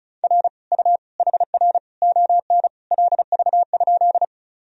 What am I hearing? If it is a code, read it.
Morse code, heard as a keyed tone, about 36 wpm, 700 Hz. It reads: R U HR ON LV?
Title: Sentences from Top 400 Words and Abbreviations 35wpm